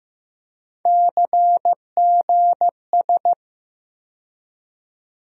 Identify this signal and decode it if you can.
Morse code, heard as a keyed tone, about 15 wpm, 700 Hz. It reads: CGS
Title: USA Call Sign Postfixes - 3 Letters 15wpm